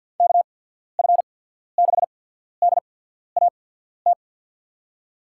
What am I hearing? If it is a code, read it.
Morse code, heard as a keyed tone, about 52 wpm, spaced as for 14 wpm, 700 Hz. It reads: KF6DAT
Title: USA Call Signs - 6 Characters - 8x Character Spacing - 50wpm